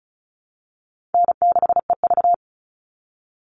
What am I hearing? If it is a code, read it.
Morse code, heard as a keyed tone, about 34 wpm, 700 Hz. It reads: N6EV
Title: Call Signs Repeated 2x - 35wpm